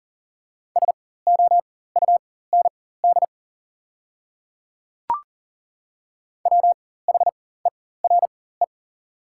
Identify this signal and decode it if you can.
Morse code, heard as a keyed tone, about 40 wpm, spaced as for 19 wpm, 690 Hz. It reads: SOUND WHERE
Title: 4x Character Spacing - Top 200 Words - 40wpm